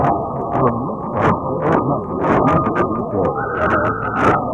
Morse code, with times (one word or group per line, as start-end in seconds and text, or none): none